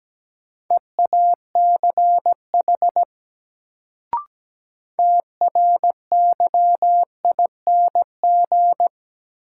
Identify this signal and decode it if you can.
Morse code, heard as a keyed tone, about 17 wpm, 700 Hz. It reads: EACH TRYING
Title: Sets of 2 Words - New Word Drill - 17wpm